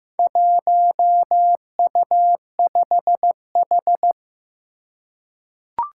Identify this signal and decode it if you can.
Morse code, heard as a keyed tone, about 15 wpm, 700 Hz. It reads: 1U5H
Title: Four Challenging Letter-Number Combinations 15wpm